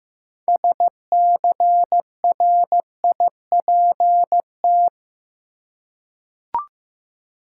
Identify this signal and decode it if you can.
Morse code, heard as a keyed tone, about 15 wpm, 700 Hz. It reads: SCRIPT
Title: Six Letter Words - 15wpm